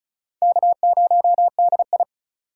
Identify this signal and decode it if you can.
Morse code, heard as a keyed tone, about 36 wpm, 700 Hz. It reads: K0DI
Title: USA Call Signs - 4 Characters - 3x Character Spacing - 35wpm